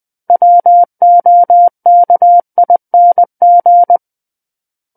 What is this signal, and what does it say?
Morse code, heard as a keyed tone, about 20 wpm, 700 Hz. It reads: WOKING
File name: 20 WPM